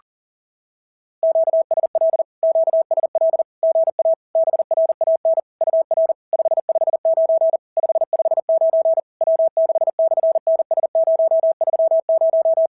Morse code, heard as a keyed tone, about 38 wpm, 660 Hz. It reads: QSL QSL GA BRAN UR 559 559 W6/NS030
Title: SOTA QSO Practice 40wpm